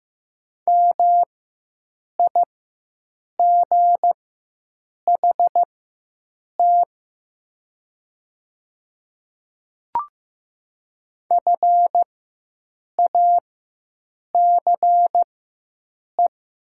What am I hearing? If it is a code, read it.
Morse code, heard as a keyed tone, about 15 wpm, spaced as for 7 wpm, 700 Hz. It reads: MIGHT FACE
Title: 4x Character Spacing - Top 300 Words - 15wpm